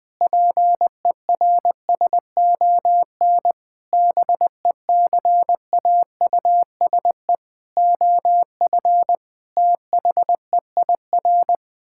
Morse code, heard as a keyed tone, about 21 wpm, 700 Hz. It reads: PERSON BECAUSE OF THEIR
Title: Sentences with 9-Word Phrases - Spoken Beginning - 20wpm